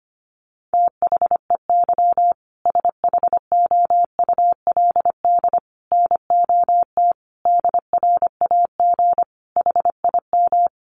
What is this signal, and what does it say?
Morse code, heard as a keyed tone, about 25 wpm, 700 Hz. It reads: THEY SHOULD NOT DRAG HIM
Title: Sentences from Top 1500 Words 25wpm